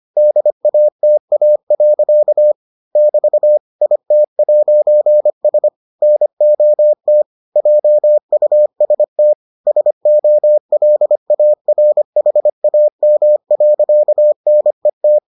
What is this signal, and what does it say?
Morse code, heard as a keyed tone, about 25 wpm, 600 Hz. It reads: DATA. <BT> IT'S NOT JUST SOLARHAM.NET